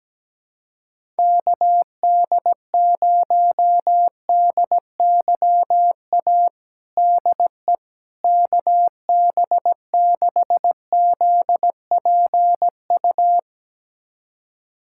Morse code, heard as a keyed tone, about 17 wpm, 700 Hz. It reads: KD0DYA DE KB6ZPU
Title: USA Callsign DE Callsign 17wpm